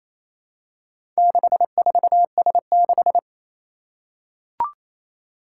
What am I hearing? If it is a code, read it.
Morse code, heard as a keyed tone, about 27 wpm, 700 Hz. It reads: 64S6
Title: Four Challenging Letter-Number Combinations 28wpm